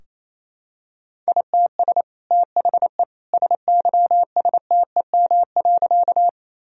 Morse code, heard as a keyed tone, about 28 wpm, 700 Hz. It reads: ITS THE SYSTEM.
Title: Sentences from Top 1000 Words - Review Part 3 - 28wpm